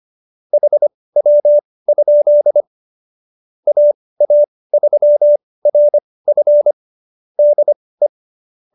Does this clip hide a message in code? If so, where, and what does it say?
Yes: Morse code from 0.5 to 8.1 s, heard as a keyed tone, about 25 wpm, spaced as for 18 wpm, 590 Hz. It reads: HW? AA3RF DE